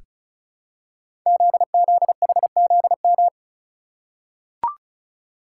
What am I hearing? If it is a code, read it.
Morse code, heard as a keyed tone, about 36 wpm, 700 Hz. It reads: ZZHZM
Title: Code Groups - 5 Letters - 35wpm